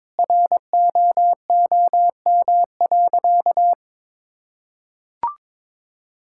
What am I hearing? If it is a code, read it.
Morse code, heard as a keyed tone, about 22 wpm, 700 Hz. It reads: ROOM.
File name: practice / 22wpm / Sentences with 2-Word Phrases - 22wpm